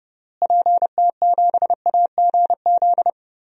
Morse code, heard as a keyed tone, about 32 wpm, 700 Hz. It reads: PT7AGZ
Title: DXCC-100 Call Signs - 6 Characters - Repeated 3x - 30wpm